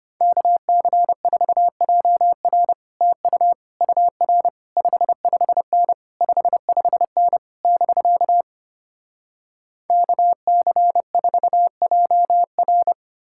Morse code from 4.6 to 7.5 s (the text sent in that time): 55N 55N